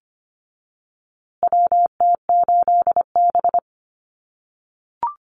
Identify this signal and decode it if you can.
Morse code, heard as a keyed tone, about 25 wpm, 700 Hz. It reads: WT8B
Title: USA Call Signs Repeated 3x 25wpm